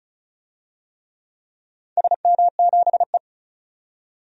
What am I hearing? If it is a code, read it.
Morse code, heard as a keyed tone, about 36 wpm, 700 Hz. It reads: SM7E
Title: DXCC-100 Call Signs - 4 Characters - 35wpm